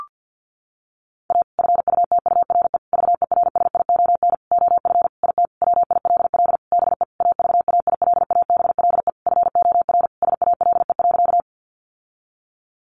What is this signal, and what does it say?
Morse code, heard as a keyed tone, about 50 wpm, 700 Hz. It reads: A FUTURE VERSION OF IT WILL BE AVAILABLE FOR SALE.
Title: Sentences from Top 1400 Words 50wpm